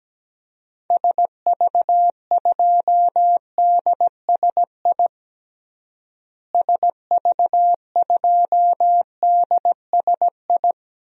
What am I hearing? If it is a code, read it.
Morse code, heard as a keyed tone, about 17 wpm, 700 Hz. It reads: SV2DSI SV2DSI
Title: DXCC-100 Call Signs - 6 Characters - Repeated 3x - 17wpm